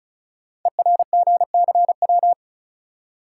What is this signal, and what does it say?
Morse code, heard as a keyed tone, about 36 wpm, 690 Hz. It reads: ERGCW
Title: Code Groups - 5 Letter:Numbers - 35wpm